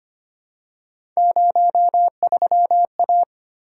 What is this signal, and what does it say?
Morse code, heard as a keyed tone, about 25 wpm, 700 Hz. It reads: 03A
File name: Three Letter-Number Combinations - 25wpm